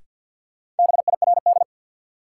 Morse code, heard as a keyed tone, about 50 wpm, 700 Hz. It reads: BIRD
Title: Sentences from Top 400 Words - New Word Drill - 50wpm